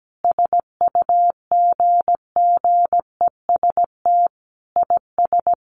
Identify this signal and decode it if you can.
Morse code, heard as a keyed tone, about 17 wpm, 700 Hz. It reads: SUGGEST IS
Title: Sentences from Top 1300 Words 17wpm